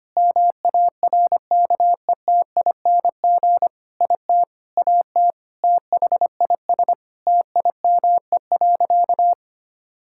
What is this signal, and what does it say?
Morse code, heard as a keyed tone, about 25 wpm, 700 Hz. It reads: MARKETING IT AT THIS TIME.
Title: Sentences from Top 1000 Words - Review Part 5 - 25wpm